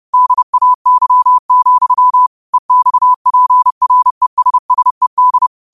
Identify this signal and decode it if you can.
Morse code, heard as a keyed tone, about 30 wpm, 1000 Hz. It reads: DAY, EXPRESSED
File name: qotdfiles / 30wpm